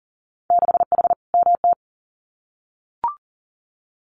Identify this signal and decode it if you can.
Morse code, heard as a keyed tone, about 40 wpm, 700 Hz. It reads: 6H MT
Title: Winter Field Day Exchanges Encore 40wpm